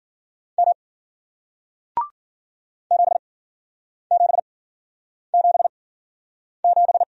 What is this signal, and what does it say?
Morse code, heard as a keyed tone, about 53 wpm, spaced as for 9 wpm, 700 Hz. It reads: M 7777